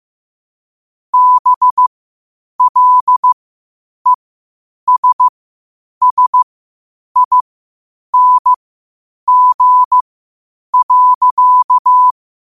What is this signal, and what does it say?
Morse code, heard as a keyed tone, about 15 wpm, spaced as for 8 wpm, 1000 Hz. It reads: BLESSING.
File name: qotdfiles / 8wpm